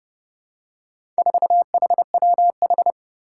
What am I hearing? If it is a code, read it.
Morse code, heard as a keyed tone, about 30 wpm, 700 Hz. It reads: 4HWH
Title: Four Challenging Letter-Number Combinations 30wpm